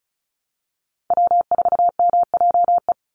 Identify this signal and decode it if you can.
Morse code, heard as a keyed tone, about 34 wpm, 700 Hz. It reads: W4MJE
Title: Call Signs Repeated 2x - 35wpm